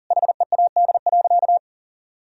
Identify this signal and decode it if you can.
Morse code, heard as a keyed tone, about 40 wpm, 700 Hz. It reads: HEAD.
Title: Sentences from Top 1000 Words - Review Part 3 - 40wpm